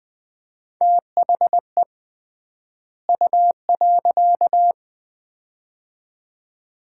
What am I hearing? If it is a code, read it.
Morse code, heard as a keyed tone, about 21 wpm, 700 Hz. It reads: THE U.
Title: 3x Word-Spacing - Sets of 2 Words - 20wpm